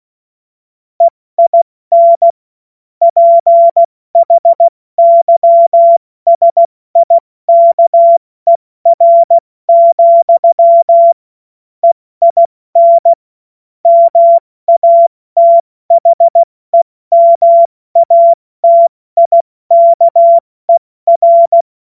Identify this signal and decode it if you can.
Morse code, heard as a keyed tone, about 16 wpm, spaced as for 14 wpm, 680 Hz. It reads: EIN PHYSIKER, EIN MATHEMATIKER
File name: Morsekurs / Witze / Witz 21. Dezember 2021, 70 F80